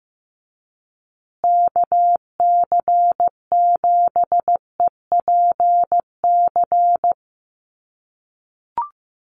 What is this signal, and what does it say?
Morse code, heard as a keyed tone, about 15 wpm, 700 Hz. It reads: KC7EPC
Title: USA Call Signs Repeated 3x 15wpm